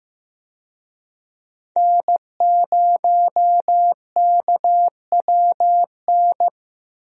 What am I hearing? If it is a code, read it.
Morse code, heard as a keyed tone, about 15 wpm, 700 Hz. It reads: N0KWN
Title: USA Call Signs - 5 Characters - Repeated 2x - 15wpm